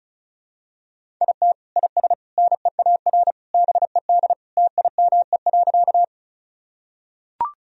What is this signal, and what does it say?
Morse code, heard as a keyed tone, about 36 wpm, 700 Hz. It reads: IT IS NEAR BED TIME.
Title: Sentences from Top 1000 Words - Review Part 1 - 35wpm